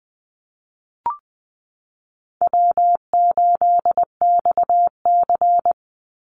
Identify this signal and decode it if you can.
Morse code, heard as a keyed tone, about 20 wpm, 700 Hz. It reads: W8XC
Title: Easy Call Signs - 20wpm